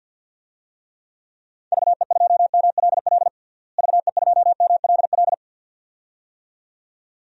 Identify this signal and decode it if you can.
Morse code, heard as a keyed tone, about 53 wpm, 700 Hz. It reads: VE2MLL VE2MLL
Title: DXCC-100 Call Signs - 6 Characters - Repeated 2x - 50wpm